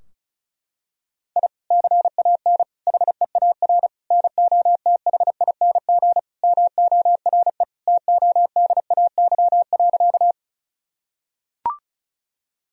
Morse code, heard as a keyed tone, about 36 wpm, 700 Hz. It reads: I CAN HEAR NOTHING MORE TODAY.
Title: Sentences from Top 1000 Words - Review Part 4 - 35wpm